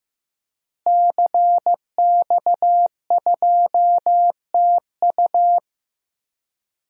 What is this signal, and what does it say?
Morse code, heard as a keyed tone, about 15 wpm, 700 Hz. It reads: CX2TU
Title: DXCC-100 Call Signs - 5 Characters - Repeated 2x - 15wpm